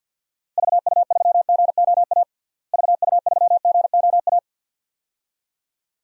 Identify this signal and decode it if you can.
Morse code, heard as a keyed tone, about 47 wpm, 700 Hz. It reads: VU3GOA VU3GOA
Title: DXCC-100 Call Signs - 6 Characters - Repeated 2x - 50wpm